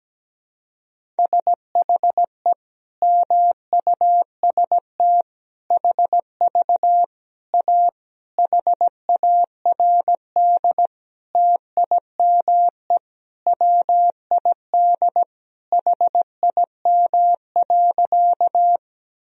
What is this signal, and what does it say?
Morse code, heard as a keyed tone, about 17 wpm, 700 Hz. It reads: SHE MUST HV A HARD TIME WID HIM.
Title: Sentences from Top 300 Words and Abbreviations 17wpm